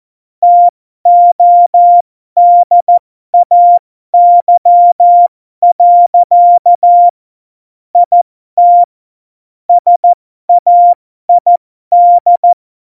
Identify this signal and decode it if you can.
Morse code, heard as a keyed tone, about 14 wpm, spaced as for 12 wpm, 700 Hz. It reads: TODAY. IT SAID
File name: Morsekurs / Witze